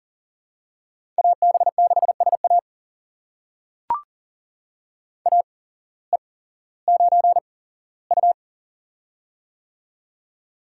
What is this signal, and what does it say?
Morse code, heard as a keyed tone, about 40 wpm, spaced as for 19 wpm, 700 Hz. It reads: AB6SA AE9U
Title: USA Call Signs - 8x Character Spacing - 40wpm